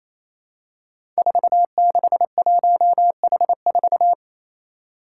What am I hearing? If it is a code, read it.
Morse code, heard as a keyed tone, about 27 wpm, 700 Hz. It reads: 461H4
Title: Five Challenging Letter-Number Combinations 28wpm